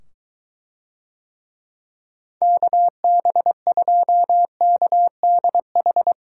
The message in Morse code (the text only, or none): KB2KDH